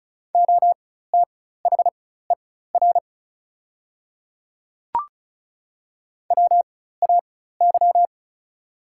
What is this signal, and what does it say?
Morse code, heard as a keyed tone, about 36 wpm, spaced as for 16 wpm, 690 Hz. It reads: OTHER WAY
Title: 4x Character Spacing - Top 100 Words - 35wpm